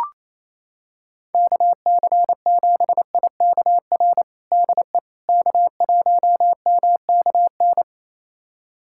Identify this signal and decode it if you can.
Morse code, heard as a keyed tone, about 28 wpm, 700 Hz. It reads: KC7IKR DE K1MKN
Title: USA Callsign DE Callsign 28wpm